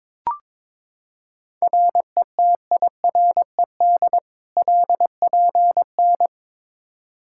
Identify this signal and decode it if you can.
Morse code, heard as a keyed tone, about 21 wpm, 700 Hz. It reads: RETIRED LPN
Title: Practice QSO - Exchange 3 - More Info - 22wpm